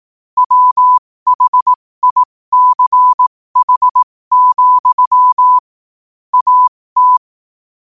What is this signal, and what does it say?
Morse code, heard as a keyed tone, about 18 wpm, spaced as for 15 wpm, 1000 Hz. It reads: WHICH, AT
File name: qotdfiles / 15wpm